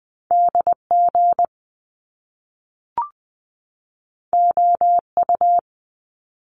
Morse code, heard as a keyed tone, about 20 wpm, 700 Hz. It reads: DG OU
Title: Top Two Letter Combinations in Words 20wpm